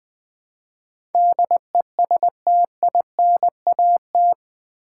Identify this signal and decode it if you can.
Morse code, heard as a keyed tone, about 20 wpm, 700 Hz. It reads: DESTINAT
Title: Build A Word - 4 Letters At A Time - 11-Letter Words - 20wpm